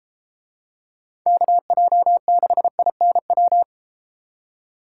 Morse code, heard as a keyed tone, about 32 wpm, 700 Hz. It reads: KJ6INW